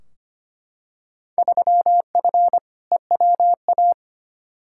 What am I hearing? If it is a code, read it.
Morse code, heard as a keyed tone, about 25 wpm, 700 Hz. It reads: 3F EWA
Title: Field Day Exchanges Encore - 25wpm